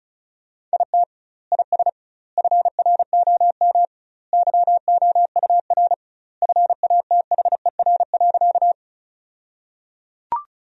Morse code, heard as a keyed tone, about 34 wpm, 700 Hz. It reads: IT IS FROM YOUR FATHER.
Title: 2x Word-Spacing - Sentences from Top 300 Words - 35wpm